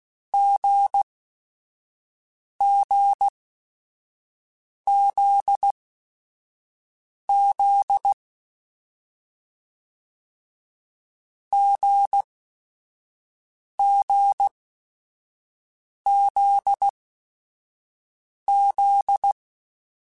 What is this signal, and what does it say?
Morse code, heard as a keyed tone, about 16 wpm, spaced as for 5 wpm, 800 Hz. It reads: GGZZ GGZZ